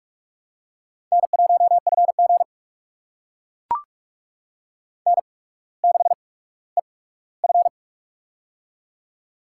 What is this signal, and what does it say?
Morse code, heard as a keyed tone, about 44 wpm, spaced as for 19 wpm, 700 Hz. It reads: N1FG N6EF
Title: USA Call Signs - 4 Characters - 8x Character Spacing - 45wpm